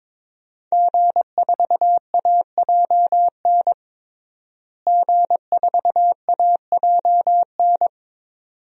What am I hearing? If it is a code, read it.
Morse code, heard as a keyed tone, about 22 wpm, 700 Hz. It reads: G4AJN G4AJN
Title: DXCC-100 Call Signs - 5 Characters - Repeated 2x - 22wpm